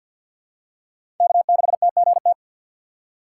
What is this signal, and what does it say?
Morse code, heard as a keyed tone, about 51 wpm, 700 Hz. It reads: K6TGT
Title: USA Call Signs - 5 Characters - 2x Character Spacing - 50wpm